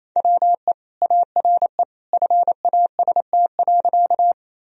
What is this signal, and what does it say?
Morse code, heard as a keyed tone, about 28 wpm, 700 Hz. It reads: WE ARE FAST.
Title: Sentences from Top 1000 Words - Review Part 2 - 28wpm